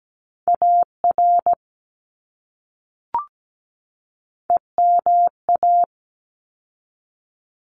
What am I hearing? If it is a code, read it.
Morse code, heard as a keyed tone, about 17 wpm, 700 Hz. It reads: AR EMA
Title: ARRL Sections 17wpm